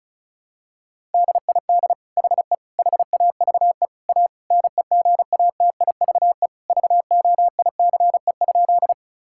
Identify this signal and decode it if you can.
Morse code, heard as a keyed tone, about 36 wpm, 700 Hz. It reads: DID HE HAVE A NEGATIVE VOICE?
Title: Sentences from Top 1000 Words - Review Part 6 - 35wpm